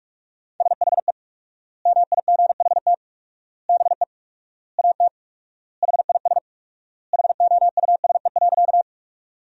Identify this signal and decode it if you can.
Morse code, heard as a keyed tone, about 41 wpm, 700 Hz. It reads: SHE MIGHT BE AT HIS HOUSE.
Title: 4x Word-Spacing - Sentences from Top 300 Words - 45wpm